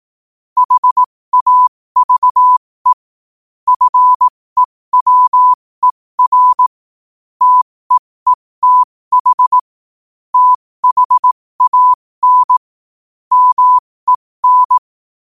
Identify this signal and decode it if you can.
Morse code, heard as a keyed tone, about 18 wpm, spaced as for 15 wpm, 1000 Hz. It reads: HAVE FEWER TEETH THAN MEN
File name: qotdfiles / 15wpm